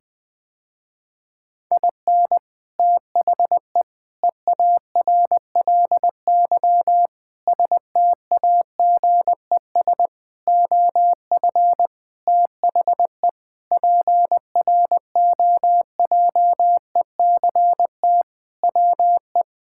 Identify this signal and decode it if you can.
Morse code, heard as a keyed tone, about 21 wpm, 700 Hz. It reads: IN THE EARLY STAGES OF THE PROJECT WE
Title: Sentences with 6-Word Phrases - Spoken Beginning - 20wpm